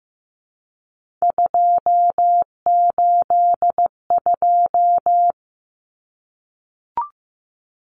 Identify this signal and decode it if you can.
Morse code, heard as a keyed tone, about 15 wpm, 700 Hz. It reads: 282
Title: Numbers - 3-Digit 15wpm